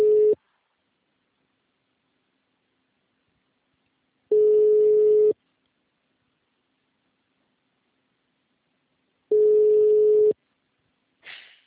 Ringback tone (Europe, 425 Hz).
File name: out- -2024